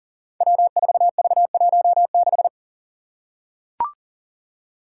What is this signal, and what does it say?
Morse code, heard as a keyed tone, about 40 wpm, 700 Hz. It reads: W4V16
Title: Five Challenging Letter-Number Combinations 40wpm